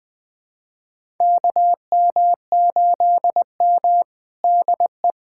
Morse code, heard as a keyed tone, about 20 wpm, 700 Hz. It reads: KM8M DE
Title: USA Callsign DE Callsign - 4 Characters - 20wpm